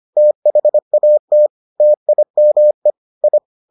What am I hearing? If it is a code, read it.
Morse code, heard as a keyed tone, about 25 wpm, 600 Hz. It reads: THAT TIME I